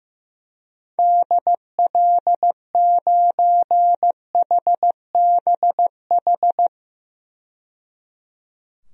Morse code, heard as a keyed tone, about 15 wpm, 700 Hz. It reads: DL9HBH
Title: DXCC-100 Call Signs - 6 Characters - 15wpm